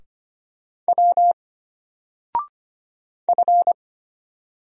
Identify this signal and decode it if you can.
Morse code, heard as a keyed tone, about 25 wpm, 700 Hz. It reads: W F
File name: Single-Letter-Number-25wpm